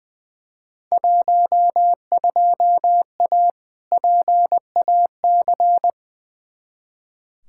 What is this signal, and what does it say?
Morse code, heard as a keyed tone, about 20 wpm, 700 Hz. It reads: 12A PAC
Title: Field Day Exchanges Encore - 20wpm